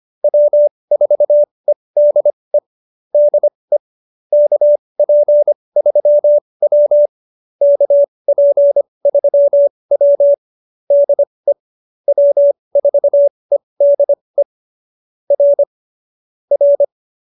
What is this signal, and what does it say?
Morse code, heard as a keyed tone, about 25 wpm, spaced as for 20 wpm, 570 Hz. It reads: W4EDE DE KP3W KP3W DE W4EDE R R
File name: QSO317 20